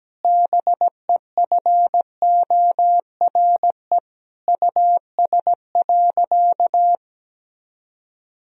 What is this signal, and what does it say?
Morse code, heard as a keyed tone, about 17 wpm, 700 Hz. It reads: BEFORE US.
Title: Sentences from Top 1000 Words - Review Part 1 - 17wpm